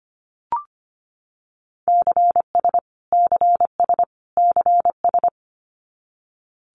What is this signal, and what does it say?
Morse code, heard as a keyed tone, about 25 wpm, 700 Hz. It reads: CS CS CS
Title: Abbreviations Repeated 3x - 25wpm